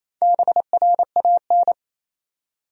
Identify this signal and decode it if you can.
Morse code, heard as a keyed tone, about 27 wpm, 700 Hz. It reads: BRAN
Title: USA Common Names - Max 5 Letters - 28wpm